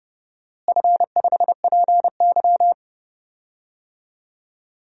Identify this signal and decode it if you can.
Morse code, heard as a keyed tone, about 30 wpm, 700 Hz. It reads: F5PY
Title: DXCC-100 Call Signs - 4 Characters - Repeated 3x - 30wpm